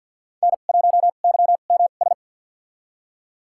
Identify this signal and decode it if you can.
Morse code, heard as a keyed tone, about 48 wpm, spaced as for 35 wpm, 700 Hz. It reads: N1YMS
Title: USA Call Signs - 5 Characters - 2x Character Spacing - 50wpm